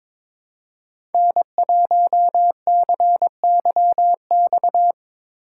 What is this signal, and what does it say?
Morse code, heard as a keyed tone, about 24 wpm, 700 Hz. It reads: N1CYX
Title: Speed Racing - USA 1x3 Call Signs 22wpm